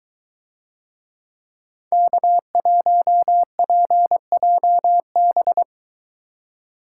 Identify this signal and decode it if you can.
Morse code, heard as a keyed tone, about 20 wpm, 700 Hz. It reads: K1PJB